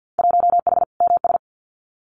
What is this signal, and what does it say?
Morse code, heard as a keyed tone, about 50 wpm, 700 Hz. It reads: 1H MS
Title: Winter Field Day Exchanges 50wpm